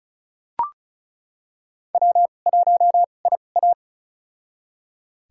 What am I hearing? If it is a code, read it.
Morse code, heard as a keyed tone, about 34 wpm, spaced as for 25 wpm, 700 Hz. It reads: W1IA
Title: USA Call Signs - 4 Characters - 2x Character Spacing - 35wpm